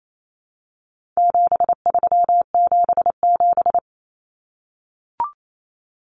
Morse code, heard as a keyed tone, about 28 wpm, 700 Hz. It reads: 7377
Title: Common Cut Numbers - 4-Digit 28wpm